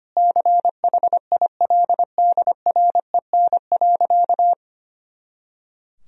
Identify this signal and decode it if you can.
Morse code, heard as a keyed tone, about 25 wpm, 700 Hz. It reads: CHILDREN.
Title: Sentences from Top 400 Words and Abbreviations 25wpm